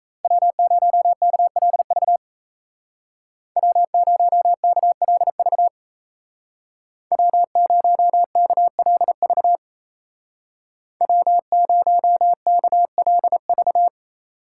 Morse code, heard as a keyed tone, about 36 wpm, 700 Hz. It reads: W0KLV W0KLV W0KLV W0KLV